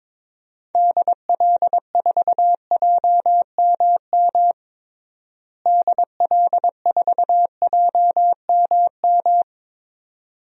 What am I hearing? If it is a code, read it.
Morse code, heard as a keyed tone, about 22 wpm, 700 Hz. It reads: DL4JMM DL4JMM
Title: DXCC-100 Call Signs - 6 Characters - Repeated 2x - 22wpm